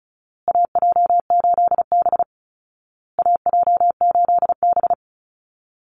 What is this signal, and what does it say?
Morse code, heard as a keyed tone, about 34 wpm, 700 Hz. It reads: AJ8B AJ8B
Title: Call Signs Repeated 2x - 35wpm